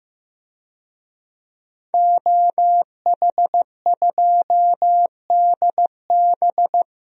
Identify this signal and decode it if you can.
Morse code, heard as a keyed tone, about 15 wpm, 700 Hz. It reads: OH2DB
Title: DXCC-100 Call Signs - 5 Characters - Repeated 2x - 15wpm